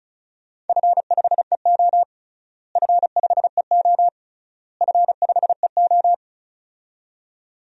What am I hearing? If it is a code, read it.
Morse code, heard as a keyed tone, about 36 wpm, 700 Hz. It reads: F5EO F5EO F5EO
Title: DXCC-100 Call Signs - 4 Characters - Repeated 3x - 35wpm